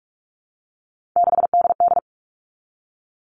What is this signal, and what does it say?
Morse code, heard as a keyed tone, about 44 wpm, 700 Hz. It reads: 6DD
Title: All Cut Numbers - 3-Digit 45wpm